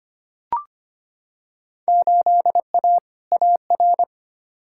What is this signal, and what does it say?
Morse code, heard as a keyed tone, about 25 wpm, 700 Hz. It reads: 8A AR
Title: Field Day Exchanges - 25wpm